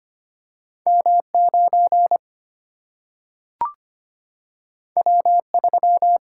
Morse code, heard as a keyed tone, about 25 wpm, 700 Hz. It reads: M9 W3
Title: Single-Letter Single-Number 25wpm